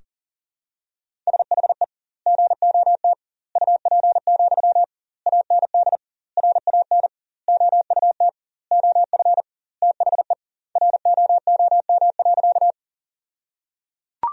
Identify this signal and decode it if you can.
Morse code, heard as a keyed tone, about 37 wpm, 700 Hz. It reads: SHE GOT UP, AND RAN OUT OF THE ROOM.
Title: 2x Word-Spacing - Sentences from Top 500 Words - 40wpm